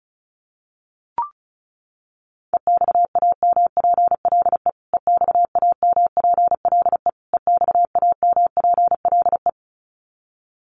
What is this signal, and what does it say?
Morse code, heard as a keyed tone, about 34 wpm, 700 Hz. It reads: EXAMPLE EXAMPLE EXAMPLE
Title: Top 500 Words Repeated 35wpm